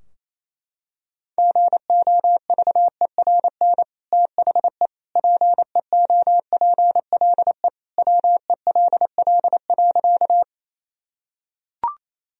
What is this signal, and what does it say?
Morse code, heard as a keyed tone, about 28 wpm, 700 Hz. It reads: GOVERN THE PEOPLE WELL.
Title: Sentences from Top 1000 Words - Review Part 2 - 28wpm